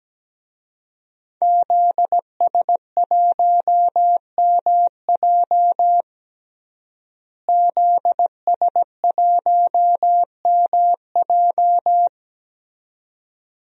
Morse code, heard as a keyed tone, about 17 wpm, 700 Hz. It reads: ZS1MJ ZS1MJ
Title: DXCC-100 Call Signs - 5 Characters - Repeated 2x - 17wpm